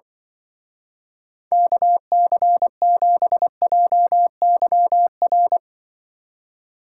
Morse code, heard as a keyed tone, about 24 wpm, 700 Hz. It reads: KC7JYR
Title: Speed Racing - USA 2x3 Call Signs 20wpm